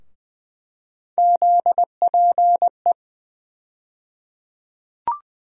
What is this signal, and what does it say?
Morse code, heard as a keyed tone, about 20 wpm, 700 Hz. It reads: ZPE
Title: USA Call Sign Postfixes - 3 Letters 20wpm